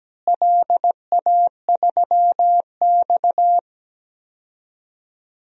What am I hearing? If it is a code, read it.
Morse code, heard as a keyed tone, about 17 wpm, 700 Hz. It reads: LA3X
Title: DXCC-100 Call Signs - 4 Characters - Repeated 2x - 17wpm